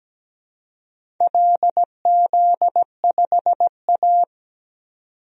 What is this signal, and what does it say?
Morse code, heard as a keyed tone, about 17 wpm, 700 Hz. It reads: LZ5A
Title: DXCC-100 Call Signs - 4 Characters - 17wpm